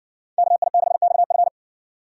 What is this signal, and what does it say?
Morse code, heard as a keyed tone, about 60 wpm, 700 Hz. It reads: KI6XF